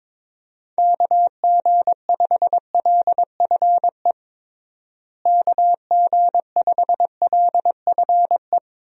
Morse code, heard as a keyed tone, about 21 wpm, 700 Hz. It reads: KG5LFE KG5LFE
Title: USA Call Signs - 6 Characters - Repeated 3x - 22wpm